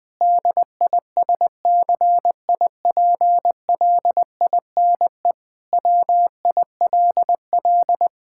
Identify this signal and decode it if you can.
Morse code, heard as a keyed tone, about 20 wpm, 700 Hz. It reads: DISCIPLINE WILL
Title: Sentences from Top 1000 Words - Review Part 6 - 20wpm